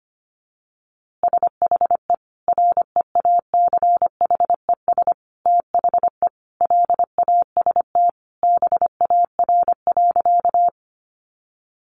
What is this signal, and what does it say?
Morse code, heard as a keyed tone, about 25 wpm, 700 Hz. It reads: SHE REACHES THE LAST BAR.
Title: Easy Phrases 25wpm